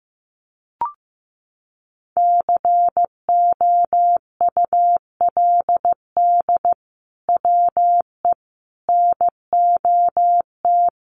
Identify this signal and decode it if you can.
Morse code, heard as a keyed tone, about 15 wpm, 700 Hz. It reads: COULD WE NOT
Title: Sentences from Top 1400 Words 15wpm